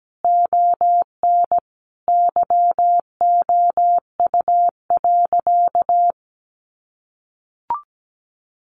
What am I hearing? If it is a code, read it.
Morse code, heard as a keyed tone, about 17 wpm, 700 Hz. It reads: ON YOU.
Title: Sentences from Top 1500 Words 17wpm